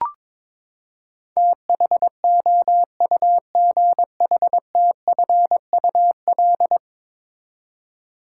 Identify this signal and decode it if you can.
Morse code, heard as a keyed tone, about 22 wpm, 700 Hz. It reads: THOUGHTFUL
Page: morse-code-ninja-practice-22wpm